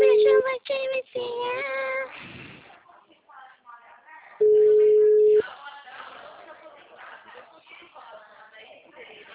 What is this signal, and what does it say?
Ringback tone (Europe, 425 Hz), with speech.
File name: out- -2010- - - .20396